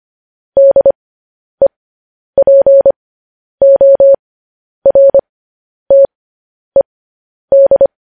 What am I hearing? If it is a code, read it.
Morse code, heard as a keyed tone, about 25 wpm, spaced as for 10 wpm, 550 Hz. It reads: DEPORTED
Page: news-headlines-in-morse-code-at-10-wpm